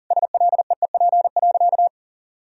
Morse code, heard as a keyed tone, about 40 wpm, 700 Hz. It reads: SLEEP.
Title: Sentences from Top 1000 Words - Review Part 2 - 40wpm